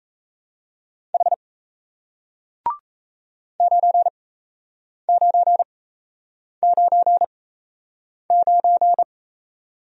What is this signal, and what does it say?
Morse code, heard as a keyed tone, about 37 wpm, spaced as for 8 wpm, 700 Hz. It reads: H 9999